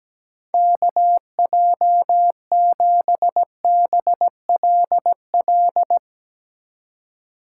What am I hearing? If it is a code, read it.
Morse code, heard as a keyed tone, about 18 wpm, 700 Hz. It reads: KJ7BLL
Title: Speed Racing - USA 2x3 Call Signs 17wpm